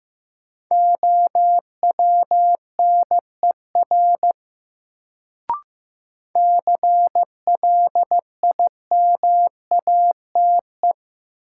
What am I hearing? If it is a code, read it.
Morse code, heard as a keyed tone, about 15 wpm, 700 Hz. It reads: OWNER CLIMATE
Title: Sentences from Top 1600 Words - New Word Drill Alt - 15wpm